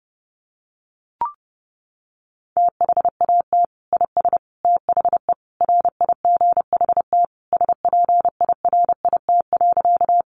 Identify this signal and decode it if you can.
Morse code, heard as a keyed tone, about 30 wpm, 700 Hz. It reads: THAT IS THE RIGHT SPIRIT.
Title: Sentences from Top 1400 Words 30wpm